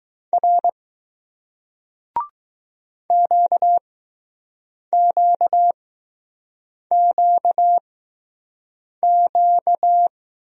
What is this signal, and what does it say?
Morse code, heard as a keyed tone, about 20 wpm, spaced as for 6 wpm, 700 Hz. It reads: R QQQQ